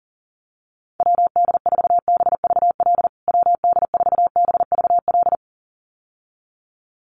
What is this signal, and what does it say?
Morse code, heard as a keyed tone, about 37 wpm, 700 Hz. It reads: WD4BVL WD4BVL
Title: USA Call Signs Repeated 2x 40wpm